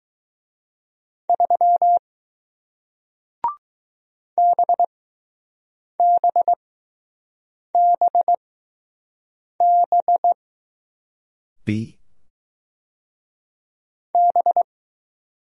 Morse code, heard as a keyed tone, about 20 wpm, spaced as for 6 wpm, 700 Hz. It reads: 3 BBBB B